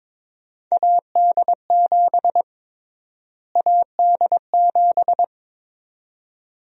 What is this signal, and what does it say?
Morse code, heard as a keyed tone, about 22 wpm, 700 Hz. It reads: AD7 AD7
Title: USA Call Sign Prefixes - Repeated 2x - 22wpm